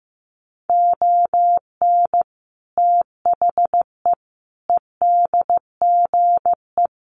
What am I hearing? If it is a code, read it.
Morse code, heard as a keyed tone, about 15 wpm, 700 Hz. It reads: ON THE EDGE
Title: Sets of 5 Words 15wpm